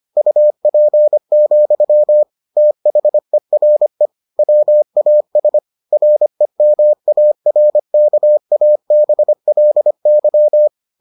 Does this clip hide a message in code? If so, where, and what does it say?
Yes: Morse code from 0.2 to 10.7 s, heard as a keyed tone, about 25 wpm, 600 Hz. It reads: UP, THERE WAS REMARKABLY